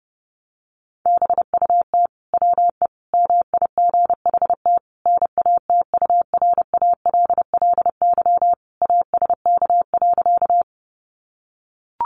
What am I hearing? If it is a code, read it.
Morse code, heard as a keyed tone, about 30 wpm, 700 Hz. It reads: BUT WE MIGHT NATURALLY ASK.
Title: Sentences from Top 1300 Words 30wpm